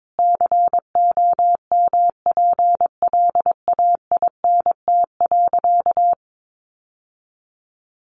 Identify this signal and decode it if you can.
Morse code, heard as a keyed tone, about 22 wpm, 700 Hz. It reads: COMPLAINT.
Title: Sentences from Top 1400 Words 22wpm